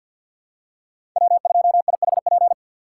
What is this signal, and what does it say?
Morse code, heard as a keyed tone, about 49 wpm, 690 Hz. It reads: W2IHP
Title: USA Call Signs - 5 Characters - 3x Character Spacing - 50wpm